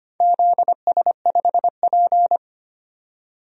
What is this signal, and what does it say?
Morse code, heard as a keyed tone, about 25 wpm, 700 Hz. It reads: ZS5P
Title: DXCC-100 Call Signs - 4 Characters - 25wpm